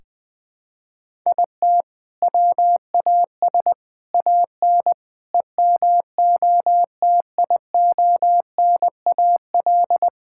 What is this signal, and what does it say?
Morse code, heard as a keyed tone, about 20 wpm, 700 Hz. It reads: IT WAS AN EMOTIONAL
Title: Sentences from Top 1000 Words - Review Part 5 - 20wpm